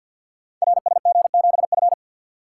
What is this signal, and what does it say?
Morse code, heard as a keyed tone, about 48 wpm, 700 Hz. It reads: RSG7F
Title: Code Groups - 5 Letter:Numbers - 50wpm